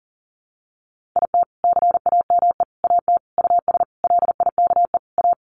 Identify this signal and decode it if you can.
Morse code, heard as a keyed tone, about 40 wpm, 700 Hz. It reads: IT CAME AT US LIKE A
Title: Sentences from Top 1400 Words 40wpm